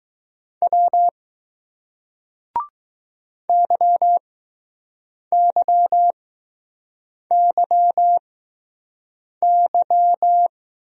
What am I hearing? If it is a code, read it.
Morse code, heard as a keyed tone, about 20 wpm, 700 Hz. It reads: W Y Y Y Y